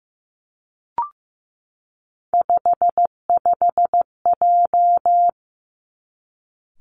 Morse code, heard as a keyed tone, about 15 wpm, 700 Hz. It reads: 55J
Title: Three Challenging Letter-Number Combinations 15wpm